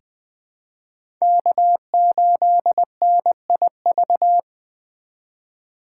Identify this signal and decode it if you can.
Morse code, heard as a keyed tone, about 21 wpm, 700 Hz. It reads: K8NIV
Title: 2023 POTA Hunter Call Signs - 20wpm